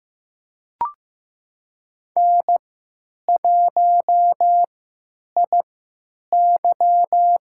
Morse code, heard as a keyed tone, about 15 wpm, spaced as for 9 wpm, 700 Hz. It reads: N1IY
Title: USA Call Signs - 3x Character Spacing - 15wpm